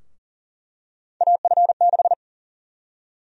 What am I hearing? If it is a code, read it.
Morse code, heard as a keyed tone, about 37 wpm, 700 Hz. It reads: AF6
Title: USA Call Sign Prefixes - 40wpm